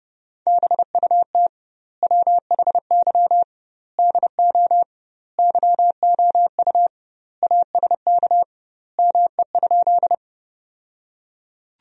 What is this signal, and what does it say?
Morse code, heard as a keyed tone, about 30 wpm, 700 Hz. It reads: BUT WHY DO YOU ASK ME?
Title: 2x Word-Spacing - Sentences from Top 200 Words - 30wpm